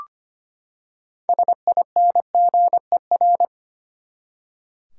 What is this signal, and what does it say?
Morse code, heard as a keyed tone, about 25 wpm, 700 Hz. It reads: SINGER
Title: Six Letter Words - 25wpm